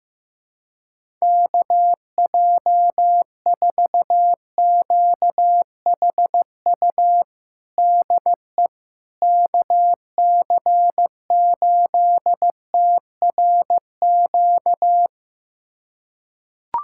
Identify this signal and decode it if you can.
Morse code, heard as a keyed tone, about 15 wpm, 700 Hz. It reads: KJ4QHU DE KC8TRQ
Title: USA Callsign DE Callsign 15wpm